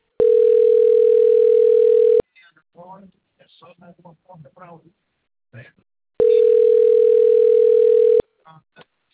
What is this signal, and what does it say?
Ringback tone (North America), 440 + 480 Hz, with speech.
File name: rg-1600-unknown- - - .19221